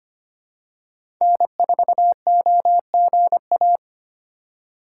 Morse code, heard as a keyed tone, about 25 wpm, 700 Hz. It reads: N4OGA
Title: USA Call Signs - 5 Characters - 2x Character Spacing - 25wpm